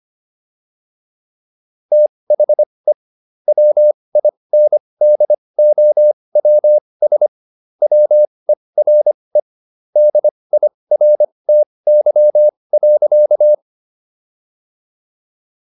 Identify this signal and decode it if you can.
Morse code, heard as a keyed tone, about 25 wpm, spaced as for 20 wpm, 600 Hz. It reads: THE WINDOWS WERE DIRTY.